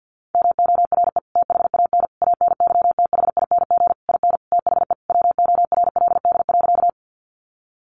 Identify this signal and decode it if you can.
Morse code, heard as a keyed tone, about 48 wpm, 700 Hz. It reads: MORE THAN ANYTHING IN THE WORLD.
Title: Sentences from Top 1000 Words 50wpm